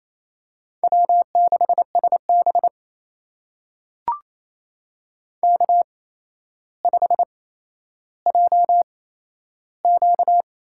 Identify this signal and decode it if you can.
Morse code, heard as a keyed tone, about 28 wpm, spaced as for 12 wpm, 700 Hz. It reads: W6SB K5JQ
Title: USA Call Signs - 4 Characters - 8x Character Spacing - 28wpm